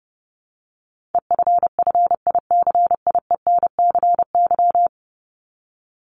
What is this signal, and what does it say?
Morse code, heard as a keyed tone, about 30 wpm, 700 Hz. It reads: EFFICIENCY
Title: Top 1501-1750 Words 30wpm